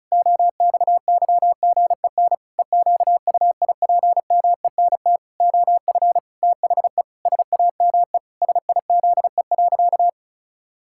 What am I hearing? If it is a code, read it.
Morse code, heard as a keyed tone, about 34 wpm, 700 Hz. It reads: OXYGEN EQUIPMENT OF THE SAME SIZE.
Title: Pangram Challenge v2 - 35wpm